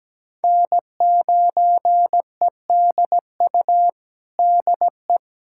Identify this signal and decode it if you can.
Morse code, heard as a keyed tone, about 17 wpm, 700 Hz. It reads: N9EDU DE
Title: USA Callsign DE Callsign - 5 Characters - 17wpm